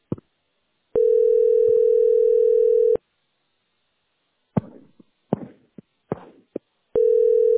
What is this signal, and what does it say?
Ringback tone (North America), 440 + 480 Hz.